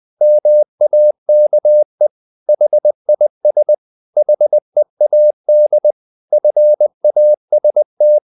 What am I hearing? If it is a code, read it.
Morse code, heard as a keyed tone, about 20 wpm, 600 Hz. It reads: MAKE HIS HEAD FAST
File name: the old man and the sea part 2 20wpm